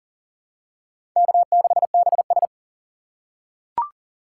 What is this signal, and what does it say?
Morse code, heard as a keyed tone, about 40 wpm, 700 Hz. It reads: K6BS
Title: USA Call Signs - 6x Character Spacing - 40wpm